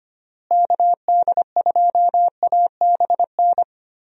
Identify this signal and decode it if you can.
Morse code, heard as a keyed tone, about 25 wpm, 700 Hz. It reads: KD2ABN
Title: USA Call Signs - 6 Characters - Repeated 2x - 25wpm